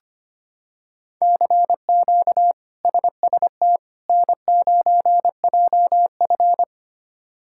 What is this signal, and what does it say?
Morse code, heard as a keyed tone, about 25 wpm, 700 Hz. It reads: CQ SST N9JF
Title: K1USN SST QSO Practice - 25wpm